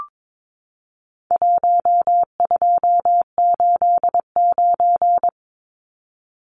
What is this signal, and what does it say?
Morse code, heard as a keyed tone, about 22 wpm, 700 Hz. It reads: 1289
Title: Numbers - Contest Style - 4-Digit 22wpm